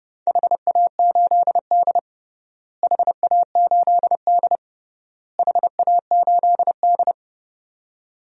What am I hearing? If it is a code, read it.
Morse code, heard as a keyed tone, about 32 wpm, 700 Hz. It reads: HA8D HA8D HA8D
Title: DXCC-100 Call Signs - 4 Characters - Repeated 3x - 30wpm